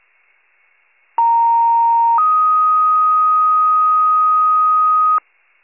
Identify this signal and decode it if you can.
Two-tone page, heard as a one-second tone, then a three-second tone, about 930 then 1250 Hz. It is heard from 1.2 to 5.2 s.